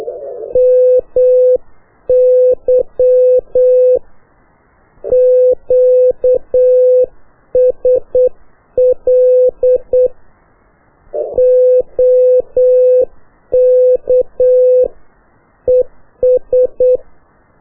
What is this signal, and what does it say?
Morse code, heard as a keyed tone, about 8 wpm, 500 Hz. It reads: MY QSL OK ES
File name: qso1-qrs